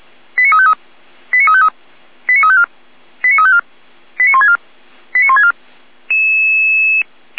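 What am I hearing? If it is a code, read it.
Five-tone selective call (ZVEI), heard as quick runs of five tones, tones of about 80 ms. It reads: 89353 89353 89354 89354 89164 89164
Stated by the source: ZVEI five-tone selective call received on SDR.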